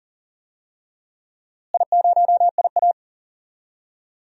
Morse code, heard as a keyed tone, about 40 wpm, 700 Hz. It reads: I0IA
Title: DXCC-100 Call Signs - 4 Characters - Repeated 2x - 40wpm